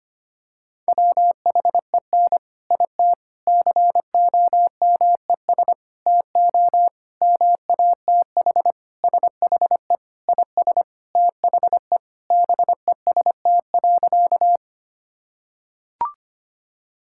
Morse code, heard as a keyed tone, about 25 wpm, 700 Hz. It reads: WHEN IT COMES TO MATH SHE IS THE BEST.
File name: practice / 25wpm / Sentences with 4-Word Phrases - 25wpm